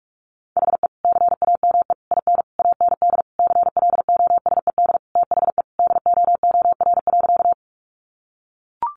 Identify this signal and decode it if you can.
Morse code, heard as a keyed tone, about 44 wpm, 700 Hz. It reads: HE CAME IN AND CLOSED THE DOOR.
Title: Sentences from Top 1400 Words 45wpm